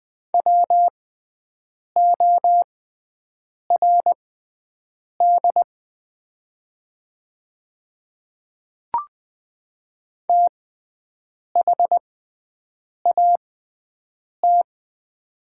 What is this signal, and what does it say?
Morse code, heard as a keyed tone, about 20 wpm, spaced as for 7 wpm, 700 Hz. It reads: WORD THAT
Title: 6x Character Spacing - Top 100 Words - 20wpm